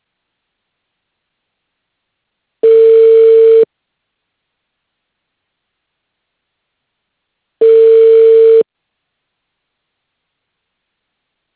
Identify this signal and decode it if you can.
Ringback tone (China, 450 Hz).